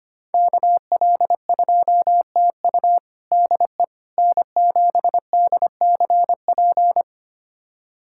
Morse code, heard as a keyed tone, about 25 wpm, 700 Hz. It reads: KL2TU DE N7DCP
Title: USA Callsign DE Callsign - 5 Characters - 25wpm